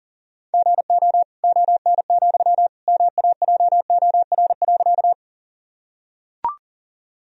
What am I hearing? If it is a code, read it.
Morse code, heard as a keyed tone, about 40 wpm, 700 Hz. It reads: GO ON, MAJOR.
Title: Sentences from Top 1000 Words - Review Part 4 - 40wpm